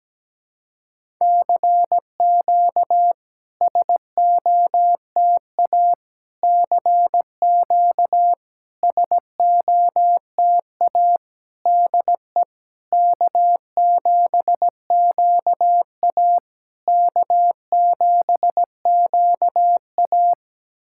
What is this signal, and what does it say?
Morse code, heard as a keyed tone, about 17 wpm, 700 Hz. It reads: CQ SOTA CQ SOTA DE K7QA K7QA
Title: SOTA QSO Practice 17wpm